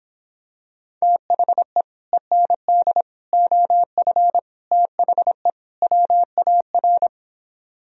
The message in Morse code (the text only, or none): THE END OF THE WAR